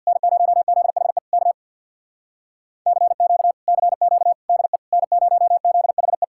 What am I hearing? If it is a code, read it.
Morse code, heard as a keyed tone, about 64 wpm, 700 Hz. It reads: N0ZHE K CQ CQ DE N0ZHE